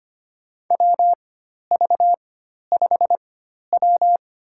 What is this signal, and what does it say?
Morse code, heard as a keyed tone, about 25 wpm, spaced as for 12 wpm, 690 Hz. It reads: WV5W
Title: USA Call Signs - 4 Characters - 4x Character Spacing - 25wpm